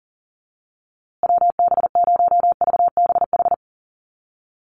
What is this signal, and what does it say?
Morse code, heard as a keyed tone, about 40 wpm, 700 Hz. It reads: WB0VBH
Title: USA Call Signs Repeated 3x 40wpm